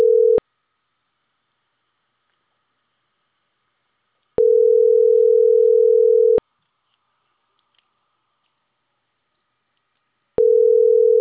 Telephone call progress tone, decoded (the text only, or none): ringback tone